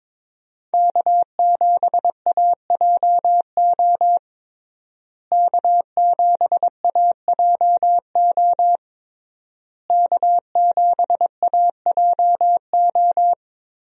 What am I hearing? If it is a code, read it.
Morse code, heard as a keyed tone, about 21 wpm, 700 Hz. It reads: K7AJO K7AJO K7AJO
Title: USA Call Signs - 5 Characters - Repeated 3x - 22wpm